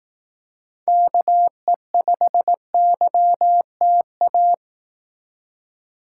Morse code, heard as a keyed tone, about 18 wpm, 700 Hz. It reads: KE5YTA
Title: Speed Racing - USA 2x3 Call Signs 15wpm